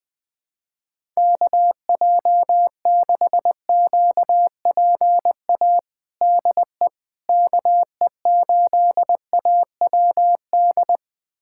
Morse code, heard as a keyed tone, about 21 wpm, 700 Hz. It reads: KJ6QPA DE KE8AWD
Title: USA Callsign DE Callsign - 6 Characters - 20wpm